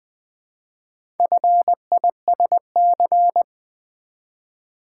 Morse code, heard as a keyed tone, about 20 wpm, 690 Hz. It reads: FISC
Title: Build A Word - 2 Letters At A Time - 6-Letter Words - 20wpm